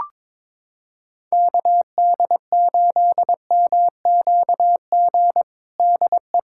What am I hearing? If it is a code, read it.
Morse code, heard as a keyed tone, about 22 wpm, 700 Hz. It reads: KD8MQG DE
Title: USA Callsign DE Callsign 22wpm